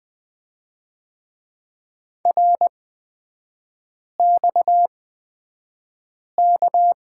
Morse code, heard as a keyed tone, about 20 wpm, spaced as for 5 wpm, 700 Hz. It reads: RXK
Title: 250 Lesson 40e - BK - Three Characters 20wpm